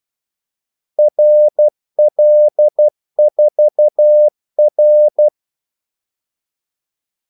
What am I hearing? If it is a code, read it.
Morse code, heard as a keyed tone, about 12 wpm, 600 Hz. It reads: RL4R
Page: 2 ekstra ordmellemrum mellem kaldesignalerne